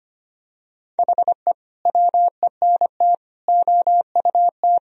Morse code, heard as a keyed tone, about 25 wpm, 700 Hz. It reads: HE WENT OUT
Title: Sentences from Top 1000 Words - Review Part 1 - 25wpm